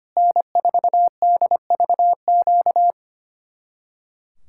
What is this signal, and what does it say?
Morse code, heard as a keyed tone, about 32 wpm, 700 Hz. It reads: N4DVQ